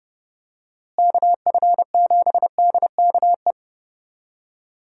Morse code, heard as a keyed tone, about 30 wpm, 700 Hz. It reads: KF7DKE
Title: USA Call Signs - 6 Characters - Repeated 3x - 30wpm